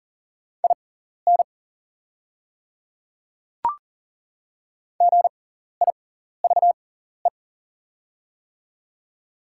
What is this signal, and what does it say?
Morse code, heard as a keyed tone, about 40 wpm, spaced as for 14 wpm, 700 Hz. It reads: IN GIVE
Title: 6x Character Spacing - Top 200 Words - 40wpm